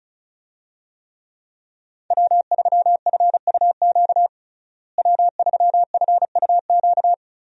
Morse code, heard as a keyed tone, about 34 wpm, 700 Hz. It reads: W3FUQ W3FUQ
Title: USA Call Signs - 5 Characters - Repeated 2x - 35wpm